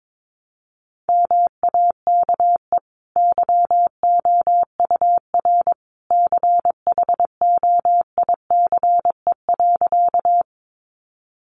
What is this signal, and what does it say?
Morse code, heard as a keyed tone, about 22 wpm, 700 Hz. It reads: MAKE YOUR CHOICE.
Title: Sentences from Top 900 Words 22wpm